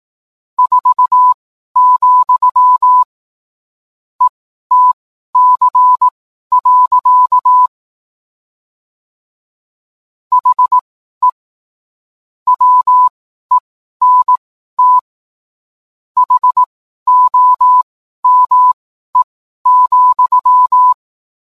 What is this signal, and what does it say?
Morse code, heard as a keyed tone, about 18 wpm, spaced as for 12 wpm, 1000 Hz. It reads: 4, ETC. HE WENT HOME,